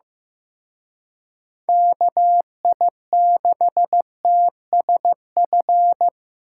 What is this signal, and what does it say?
Morse code, heard as a keyed tone, about 18 wpm, 700 Hz. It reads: KI6TSF
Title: Speed Racing - USA 2x3 Call Signs 15wpm